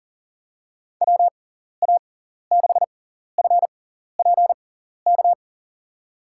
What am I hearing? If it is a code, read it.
Morse code, heard as a keyed tone, about 40 wpm, spaced as for 14 wpm, 700 Hz. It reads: WA6FPK
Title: USA Call Signs - 6 Characters - 6x Character Spacing - 40wpm